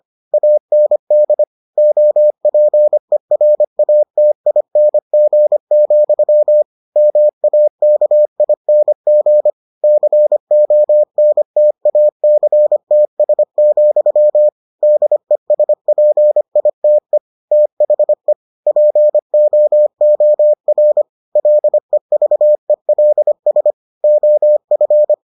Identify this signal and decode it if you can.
Morse code, heard as a keyed tone, about 25 wpm, 600 Hz. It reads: AND OPERATING, MAKING CONTACTS, DESPITE THE POOR LEVELS OF